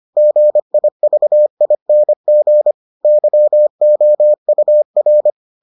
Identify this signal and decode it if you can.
Morse code, heard as a keyed tone, about 25 wpm, 600 Hz. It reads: GIVING YOUR